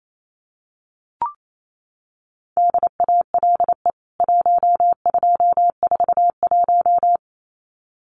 Morse code, heard as a keyed tone, about 28 wpm, 700 Hz. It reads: DALE 1241
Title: CWT Exchange 28wpm